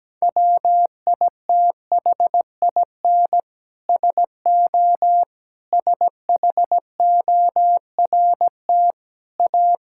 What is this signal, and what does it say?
Morse code, heard as a keyed tone, about 17 wpm, 700 Hz. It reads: WITHIN SO SHORT A
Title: Sentences from Top 1000 Words - Review Part 5 - 17wpm